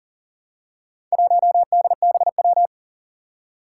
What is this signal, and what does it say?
Morse code, heard as a keyed tone, about 38 wpm, 700 Hz. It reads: 1DBW
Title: Four Challenging Letter-Number Combinations 40wpm